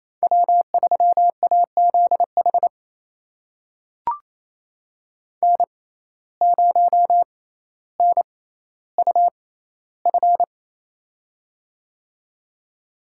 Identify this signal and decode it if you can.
Morse code, heard as a keyed tone, about 28 wpm, spaced as for 14 wpm, 700 Hz. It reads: W3AZH N0NUF
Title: USA Call Signs - 5 Characters - 6x Character Spacing - 28wpm